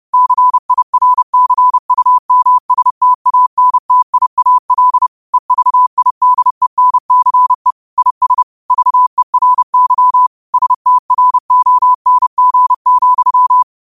Morse code, heard as a keyed tone, about 30 wpm, 1000 Hz. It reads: CIRCUMSTANTIAL EVIDENCE IS VERY STRONG,
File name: qotdfiles / 30wpm